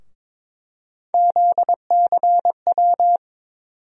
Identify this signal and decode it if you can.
Morse code, heard as a keyed tone, about 22 wpm, 700 Hz. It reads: ZCW
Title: USA Call Sign Postfixes - 3 Letters 22wpm